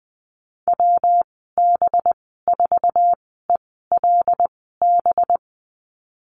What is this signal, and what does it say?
Morse code, heard as a keyed tone, about 20 wpm, spaced as for 14 wpm, 700 Hz. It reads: WB4ELB
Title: 186 Lesson 30g - Letter b - Call Signs 20wpm